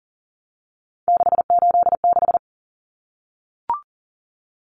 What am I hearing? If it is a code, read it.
Morse code, heard as a keyed tone, about 40 wpm, 700 Hz. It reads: 686
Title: Common Cut Numbers - 3-Digit 40wpm